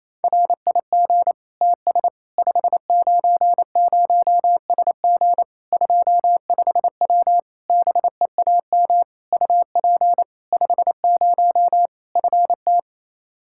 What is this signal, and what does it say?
Morse code, heard as a keyed tone, about 28 wpm, 700 Hz. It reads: RIG TS 590SG 25W BEAM UP 50 FT
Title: Practice QSO - Exchange 2 - Info - 28wpm